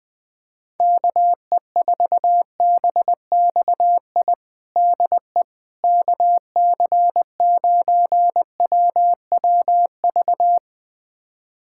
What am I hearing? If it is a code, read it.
Morse code, heard as a keyed tone, about 21 wpm, 700 Hz. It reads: KE4BXI DE KC9WWV
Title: USA Callsign DE Callsign 20wpm